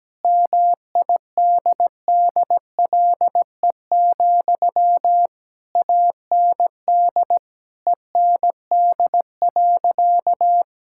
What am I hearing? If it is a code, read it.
Morse code, heard as a keyed tone, about 17 wpm, 700 Hz. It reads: MIDDLE, AND END.
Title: Sentences from Top 1000 Words - Review Part 6 - 17wpm